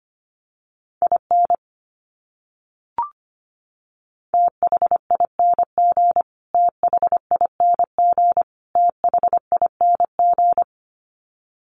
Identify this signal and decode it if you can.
Morse code, heard as a keyed tone, about 25 wpm, 700 Hz. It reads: IN THING THING THING
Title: Top 200 Words Repeated 25wpm